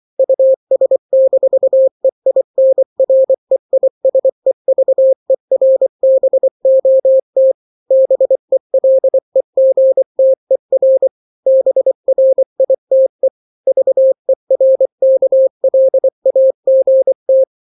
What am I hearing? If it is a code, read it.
Morse code, heard as a keyed tone, about 24 wpm, 530 Hz. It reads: US-EINREISEVERBOT BELEGTER BRITE VERKLAGT
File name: daily / 23wpm